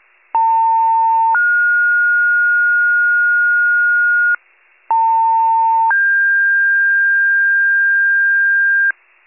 Two-tone page, heard as a one-second tone, then a three-second tone, about 900 then 1430 Hz, about 900 then 1600 Hz, with noise. It is heard from 0.3 to 8.9 s.